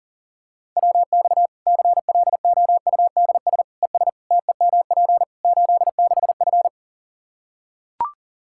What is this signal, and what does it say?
Morse code, heard as a keyed tone, about 37 wpm, 700 Hz. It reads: WX CLOUDS ES TEMP 86F
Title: Practice QSO - Exchange 2 - Info - 40wpm